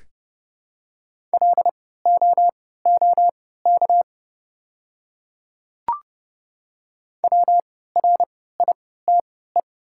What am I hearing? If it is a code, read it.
Morse code, heard as a keyed tone, about 31 wpm, spaced as for 17 wpm, 700 Hz. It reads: LOOK WRITE
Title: 3x Character Spacing - Top 100 Words - 30wpm